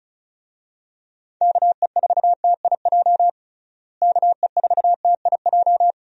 Morse code, heard as a keyed tone, about 36 wpm, 700 Hz. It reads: KE4TIJ KE4TIJ
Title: USA Call Signs - 6 Characters - Repeated 2x - 35wpm